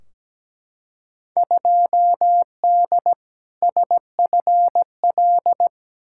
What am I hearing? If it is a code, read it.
Morse code, heard as a keyed tone, about 17 wpm, 700 Hz. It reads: 2D SFL
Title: Field Day Exchanges Encore - 17wpm